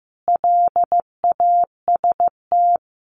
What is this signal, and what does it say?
Morse code, heard as a keyed tone, about 15 wpm, 700 Hz. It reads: LAST
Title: Sets of 5 Words 15wpm